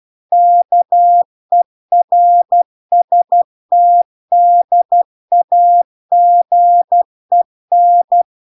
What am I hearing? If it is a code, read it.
Morse code, heard as a keyed tone, about 12 wpm, 700 Hz. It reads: KERSTDAGEN